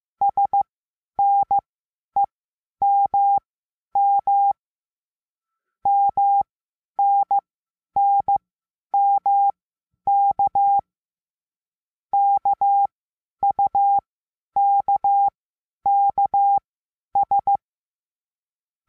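Morse code, heard as a keyed tone, about 15 wpm, spaced as for 10 wpm, 800 Hz. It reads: SNEMM MNNMK KUKKS
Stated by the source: Morse lesson adding the letter N, sent as five-letter code groups.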